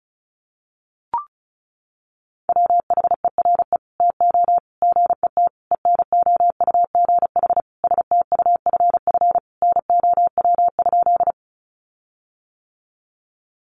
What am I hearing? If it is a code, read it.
Morse code, heard as a keyed tone, about 34 wpm, 700 Hz. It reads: WHERE TO GET ENOUGH STUFF NOW?
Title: Sentences from Top 1400 Words 35wpm